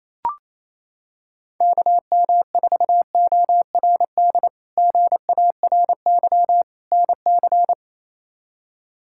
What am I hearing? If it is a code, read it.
Morse code, heard as a keyed tone, about 28 wpm, 700 Hz. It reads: KM4ORD GARY NC
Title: K1USN SST QSO Practice - 28wpm